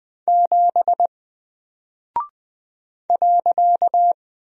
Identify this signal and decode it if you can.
Morse code, heard as a keyed tone, about 20 wpm, 700 Hz. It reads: 7 .
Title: Single Character (Letter:Number:Punctuation:Prosign) 20wpm